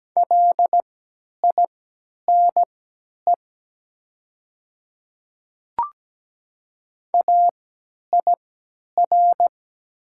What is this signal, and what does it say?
Morse code, heard as a keyed tone, about 17 wpm, spaced as for 10 wpm, 700 Hz. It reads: LINE AIR
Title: 3x Character Spacing - Top 300 Words - 17wpm